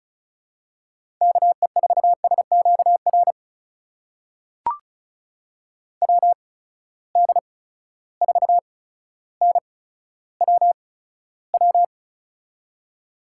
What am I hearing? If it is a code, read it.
Morse code, heard as a keyed tone, about 36 wpm, spaced as for 15 wpm, 700 Hz. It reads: KE4SQR WD4NWW
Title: USA Call Signs - 6 Characters - 8x Character Spacing - 35wpm